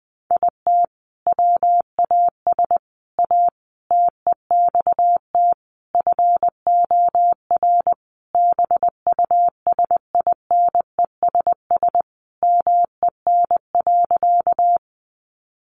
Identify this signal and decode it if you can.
Morse code, heard as a keyed tone, about 20 wpm, 700 Hz. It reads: IT WAS A TEXT FOR BUSINESS MEN.
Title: Sentences from Top 1200 Words 20wpm